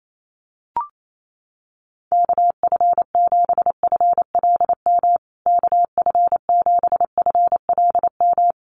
Morse code, heard as a keyed tone, about 28 wpm, 700 Hz. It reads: KF7FLM KF7FLM
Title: USA Call Signs Repeated 3x 28wpm